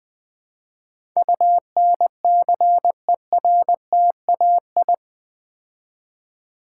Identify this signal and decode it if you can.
Morse code, heard as a keyed tone, about 20 wpm, 700 Hz. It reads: UNCERTAI
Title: Build A Word - 4 Letters At A Time - 11-Letter Words - 20wpm